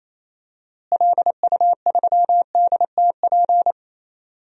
Morse code, heard as a keyed tone, about 28 wpm, 700 Hz. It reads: LU3DTP
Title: DXCC-100 Call Signs - 6 Characters - Repeated 3x - 28wpm